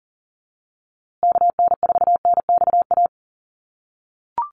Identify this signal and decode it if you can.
Morse code, heard as a keyed tone, about 40 wpm, 700 Hz. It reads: KN4NXA
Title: USA Call Signs Repeated 3x 40wpm